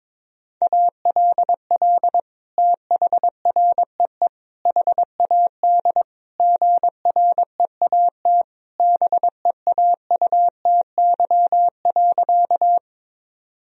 Morse code, heard as a keyed tone, about 22 wpm, 700 Hz. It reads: ALL THREE HAD GREAT BEAUTY.
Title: Sentences from Top 1000 Words - Review Part 2 - 22wpm